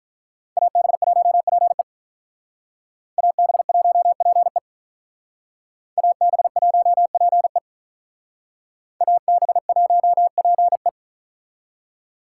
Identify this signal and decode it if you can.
Morse code, heard as a keyed tone, about 45 wpm, 700 Hz. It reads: AB1PE AB1PE AB1PE AB1PE